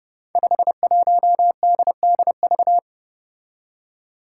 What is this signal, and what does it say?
Morse code, heard as a keyed tone, about 30 wpm, 700 Hz. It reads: 51DDV
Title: Five Challenging Letter-Number Combinations 30wpm